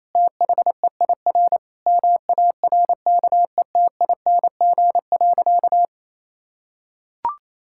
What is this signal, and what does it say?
Morse code, heard as a keyed tone, about 28 wpm, 700 Hz. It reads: THEIR MARKETING.
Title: Sentences from Top 1000 Words - Review Part 5 - 28wpm